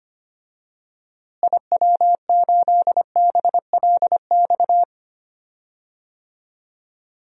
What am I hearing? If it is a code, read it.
Morse code, heard as a keyed tone, about 25 wpm, 700 Hz. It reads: IW8BLX
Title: DXCC-100 Call Signs - 6 Characters - Repeated 3x - 25wpm